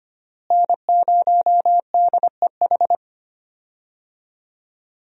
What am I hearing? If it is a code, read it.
Morse code, heard as a keyed tone, about 25 wpm, 700 Hz. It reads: N0DEH
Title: USA Call Signs - 5 Characters - Repeated 3x - 25wpm